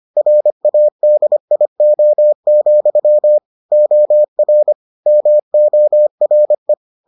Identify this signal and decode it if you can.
Morse code, heard as a keyed tone, about 25 wpm, 600 Hz. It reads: RADIO, OR MORE